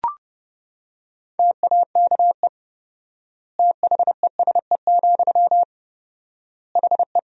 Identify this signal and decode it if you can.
Morse code, heard as a keyed tone, about 30 wpm, 700 Hz. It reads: TAKE THESE, HE
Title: 4x Word-Spacing - Sentences from Top 300 Words - 30wpm